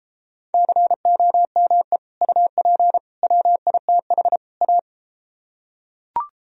Morse code, heard as a keyed tone, about 33 wpm, 700 Hz. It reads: COME UP WITH A